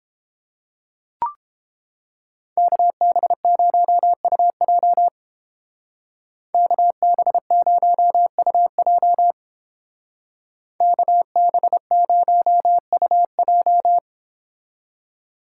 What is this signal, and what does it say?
Morse code, heard as a keyed tone, about 30 wpm, 700 Hz. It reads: KB0UJ KB0UJ KB0UJ